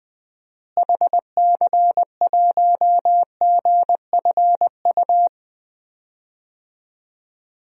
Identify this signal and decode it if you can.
Morse code, heard as a keyed tone, about 20 wpm, 700 Hz. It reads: HC1GFU
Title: DXCC-100 Call Signs - 6 Characters - Repeated 3x - 20wpm